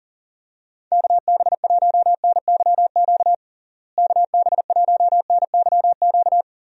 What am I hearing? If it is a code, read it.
Morse code, heard as a keyed tone, about 40 wpm, 700 Hz. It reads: KB1NYQ KB1NYQ
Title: USA Call Signs - 6 Characters - Repeated 2x - 40wpm